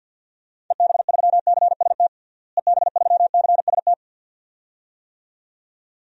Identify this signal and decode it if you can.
Morse code, heard as a keyed tone, about 51 wpm, 700 Hz. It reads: EB3CST EB3CST
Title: DXCC-100 Call Signs - 6 Characters - Repeated 2x - 50wpm